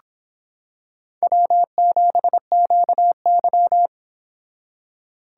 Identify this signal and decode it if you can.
Morse code, heard as a keyed tone, about 25 wpm, 700 Hz. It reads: W7QY